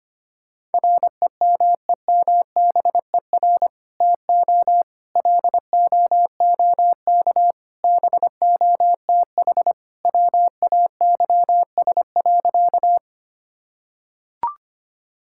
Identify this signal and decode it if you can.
Morse code, heard as a keyed tone, about 25 wpm, 700 Hz. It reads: REMEMBER TO LOOK BOTH WAYS.
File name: practice / 25wpm / Sentences with 2-Word Phrases - 25wpm